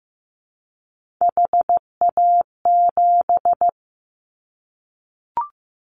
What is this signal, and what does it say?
Morse code, heard as a keyed tone, about 15 wpm, 700 Hz. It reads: HA7
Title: Callsign Prefixes for Top 25 DXCC - 15wpm